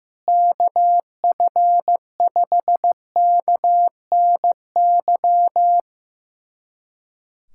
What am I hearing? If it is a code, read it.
Morse code, heard as a keyed tone, about 15 wpm, 700 Hz. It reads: KF5KNY
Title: USA Callsign DE Callsign - 6 Characters - 15wpm